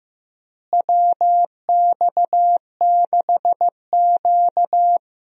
Morse code, heard as a keyed tone, about 15 wpm, 700 Hz. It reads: WX6Q
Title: USA Call Signs - 4 Characters - Repeated 2x - 15wpm